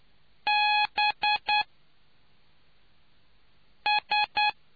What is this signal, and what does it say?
Morse code, heard as a keyed tone, about 10 wpm, spaced as for 3 wpm, 840 Hz. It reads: BS